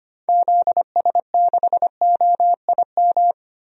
Morse code, heard as a keyed tone, about 25 wpm, 700 Hz. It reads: ZS6OIM
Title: DXCC-100 Call Signs - 6 Characters - 25wpm